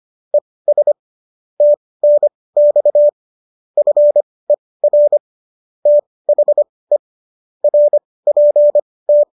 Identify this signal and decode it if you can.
Morse code, heard as a keyed tone, about 25 wpm, spaced as for 18 wpm, 590 Hz. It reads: ES TNX FER THE RPT
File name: qso215 18